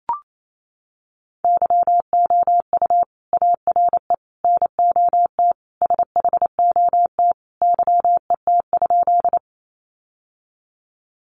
Morse code, heard as a keyed tone, about 28 wpm, 700 Hz. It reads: YOU ARE NOT SHOT YET?
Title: Sentences from Top 1100 Words 28wpm